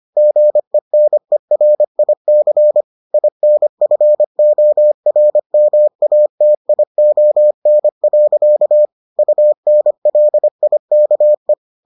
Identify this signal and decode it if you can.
Morse code, heard as a keyed tone, about 25 wpm, 600 Hz. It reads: GENERIC INFORMATION. UNLIKE